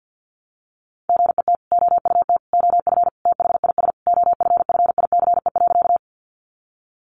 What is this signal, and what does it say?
Morse code, heard as a keyed tone, about 50 wpm, 700 Hz. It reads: GET OUT OF THIS OFFICE.
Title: Sentences from Top 1000 Words 50wpm